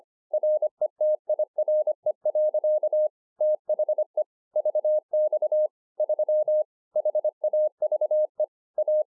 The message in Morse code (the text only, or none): RETIRE. THE VX 3 HAVE A